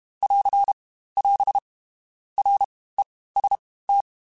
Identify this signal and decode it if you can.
Morse code, heard as a keyed tone, about 32 wpm, spaced as for 18 wpm, 800 Hz. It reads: <AR> <AS> REST
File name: Basic Letter Groups By7Groups / JZ17 / JZ17@30